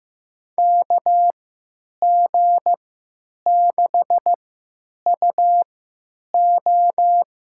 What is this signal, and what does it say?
Morse code, heard as a keyed tone, about 15 wpm, spaced as for 9 wpm, 700 Hz. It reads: KG6UO
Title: USA Call Signs - 3x Character Spacing - 15wpm